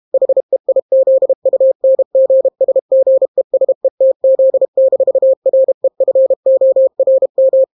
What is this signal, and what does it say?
Morse code, heard as a keyed tone, about 32 wpm, 530 Hz. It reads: HEIZUNGSGESETZ-REFORM